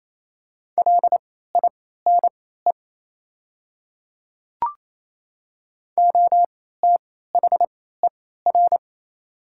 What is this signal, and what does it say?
Morse code, heard as a keyed tone, about 28 wpm, spaced as for 16 wpm, 700 Hz. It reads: LINE OTHER
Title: 3x Character Spacing - Top 300 Words - 28wpm